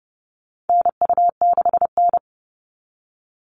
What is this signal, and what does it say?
Morse code, heard as a keyed tone, about 30 wpm, 700 Hz. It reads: NU6N
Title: Call Signs Repeated 3x - 30wpm